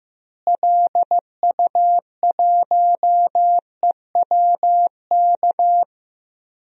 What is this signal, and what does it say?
Morse code, heard as a keyed tone, about 15 wpm, 700 Hz. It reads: LU1EWK
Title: DXCC-100 Call Signs - 6 Characters - Repeated 3x - 15wpm